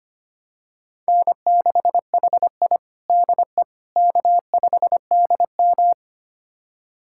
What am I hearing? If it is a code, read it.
Morse code, heard as a keyed tone, about 25 wpm, 700 Hz. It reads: N6HI DE K5DM
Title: USA Callsign DE Callsign - 4 Characters - 25wpm